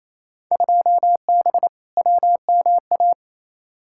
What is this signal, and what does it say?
Morse code, heard as a keyed tone, about 28 wpm, 700 Hz. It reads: 2B WMA
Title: Field Day Exchanges Encore - 28wpm